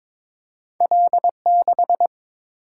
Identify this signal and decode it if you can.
Morse code, heard as a keyed tone, about 22 wpm, 700 Hz. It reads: L6
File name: practice / 22wpm / Two Letter-Number Combinations - 22wpm